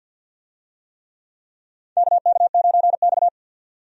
Morse code, heard as a keyed tone, about 52 wpm, 700 Hz. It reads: KK9X
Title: USA Call Signs - 4 Characters - 50wpm